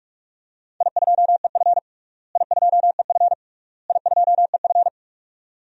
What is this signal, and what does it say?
Morse code, heard as a keyed tone, about 46 wpm, 700 Hz. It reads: I2EF I2EF I2EF
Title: DXCC-100 Call Signs - 4 Characters - Repeated 3x - 45wpm